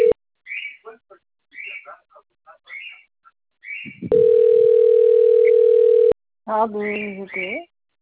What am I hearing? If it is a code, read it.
Ringback tone (North America), 440 + 480 Hz, with speech.
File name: out- - - - - .11380